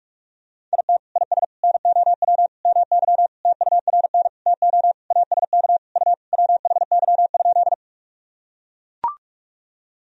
Morse code, heard as a keyed tone, about 41 wpm, 700 Hz. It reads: IT IS NOW MY TURN TO ASK U WHY?
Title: Sentences from Top 200 Words and Abbreviations 45wpm